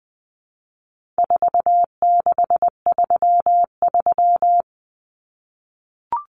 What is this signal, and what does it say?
Morse code, heard as a keyed tone, about 20 wpm, 700 Hz. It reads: 4633
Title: Common Cut Numbers - 4-Digit 20wpm